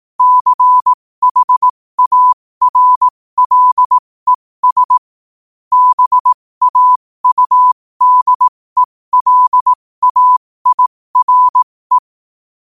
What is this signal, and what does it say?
Morse code, heard as a keyed tone, about 18 wpm, spaced as for 15 wpm, 1000 Hz. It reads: CHARLES BAUDELAIRE
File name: qotdfiles / 15wpm